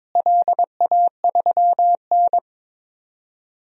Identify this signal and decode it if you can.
Morse code, heard as a keyed tone, about 22 wpm, 700 Hz. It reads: LA3N
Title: DXCC-100 Call Signs - 4 Characters - Repeated 2x - 22wpm